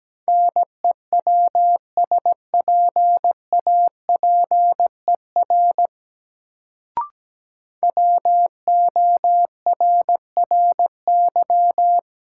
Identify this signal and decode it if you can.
Morse code, heard as a keyed tone, about 17 wpm, 700 Hz. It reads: NEWSPAPER WORRY
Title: Sentences from Top 1100 Words - New Word Drill Alt - 17wpm